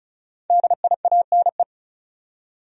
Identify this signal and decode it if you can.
Morse code, heard as a keyed tone, about 36 wpm, 700 Hz. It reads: DIANE
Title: USA Common Names - 35wpm